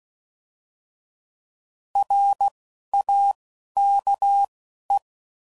Morse code, heard as a keyed tone, about 16 wpm, 800 Hz. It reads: R A K E
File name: Prefixes / By First Letter / Rs